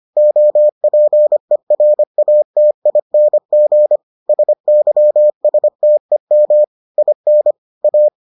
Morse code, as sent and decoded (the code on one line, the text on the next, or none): --- .--. .-. .- - .. -. --. / ... -.-- ... - . -- / .. -. / .-
OPERATING SYSTEM IN A